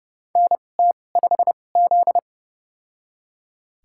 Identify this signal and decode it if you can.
Morse code, heard as a keyed tone, about 31 wpm, spaced as for 22 wpm, 700 Hz. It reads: NT5Z
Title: USA Call Signs - 4 Characters - 2x Character Spacing - 30wpm